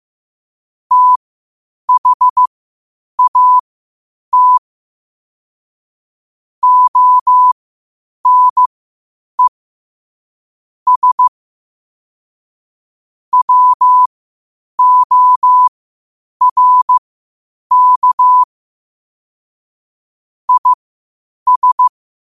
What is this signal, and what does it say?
Morse code, heard as a keyed tone, about 15 wpm, spaced as for 8 wpm, 1000 Hz. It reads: THAT ONE S WORK IS
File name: qotdfiles / 8wpm